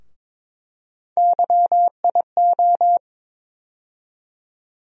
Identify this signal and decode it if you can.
Morse code, heard as a keyed tone, about 22 wpm, 700 Hz. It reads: YIO
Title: USA Call Sign Postfixes - 3 Letters 22wpm